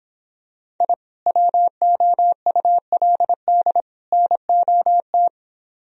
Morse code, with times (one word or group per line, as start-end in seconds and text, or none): 0.8-0.9 I
1.3-3.8 WOULD
4.1-5.3 NOT